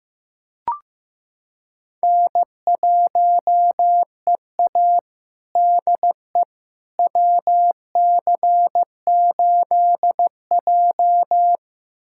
Morse code, heard as a keyed tone, about 16 wpm, 700 Hz. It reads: N1EA DE WC8J
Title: USA Callsign DE Callsign - 4 Characters - 15wpm